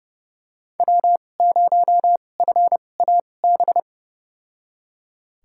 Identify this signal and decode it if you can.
Morse code, heard as a keyed tone, about 30 wpm, spaced as for 22 wpm, 700 Hz. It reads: W0FAB
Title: USA Call Signs - 5 Characters - 2x Character Spacing - 30wpm